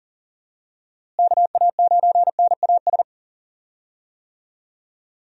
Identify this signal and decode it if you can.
Morse code, heard as a keyed tone, about 40 wpm, 700 Hz. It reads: KA9NAS
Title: USA Call Signs - 6 Characters - Repeated 2x - 40wpm